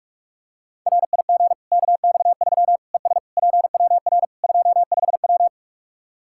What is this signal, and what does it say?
Morse code, heard as a keyed tone, about 46 wpm, 700 Hz. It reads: RIG KX3 ES PWR 25W
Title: Practice QSO - Exchange 2 - Info - 45wpm